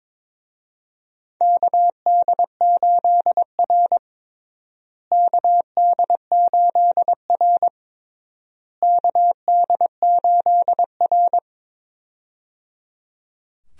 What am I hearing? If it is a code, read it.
Morse code, heard as a keyed tone, about 23 wpm, 700 Hz. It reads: KD8R KD8R KD8R
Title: USA Call Signs - 4 Characters - Repeated 3x - 22wpm